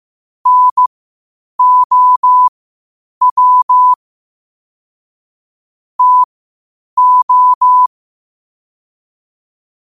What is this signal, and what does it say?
Morse code, heard as a keyed tone, about 15 wpm, spaced as for 8 wpm, 1000 Hz. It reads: NOW TO